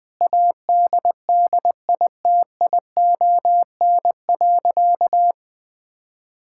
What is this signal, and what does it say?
Morse code, heard as a keyed tone, about 20 wpm, 700 Hz. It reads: ADDITION.
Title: Sentences from Top 1000 Words - Review Part 6 - 20wpm